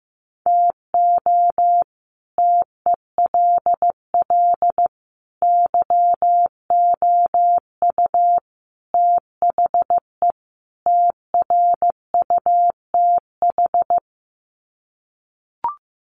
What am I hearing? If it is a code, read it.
Morse code, heard as a keyed tone, about 15 wpm, 700 Hz. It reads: TO TELL YOU THE TRUTH
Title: Sets of 5 Words 15wpm